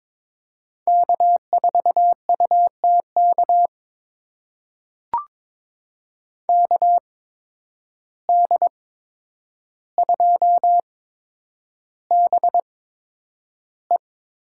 Morse code, heard as a keyed tone, about 22 wpm, spaced as for 9 wpm, 700 Hz. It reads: K4UTK KD2BE
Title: USA Call Signs - 5 Characters - 8x Character Spacing - 22wpm